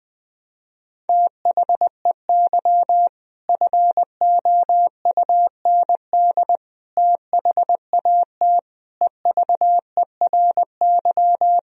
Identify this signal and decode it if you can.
Morse code, heard as a keyed tone, about 20 wpm, 700 Hz. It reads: THEY FOUND THAT EVERY
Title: Sentences from Top 1000 Words - Review Part 4 - 20wpm